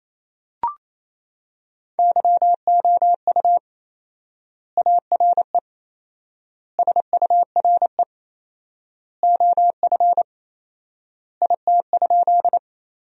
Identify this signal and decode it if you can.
Morse code, heard as a keyed tone, about 28 wpm, 700 Hz. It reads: YOU ARE SURE OF IT?
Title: 4x Word-Spacing - Sentences from Top 300 Words - 28wpm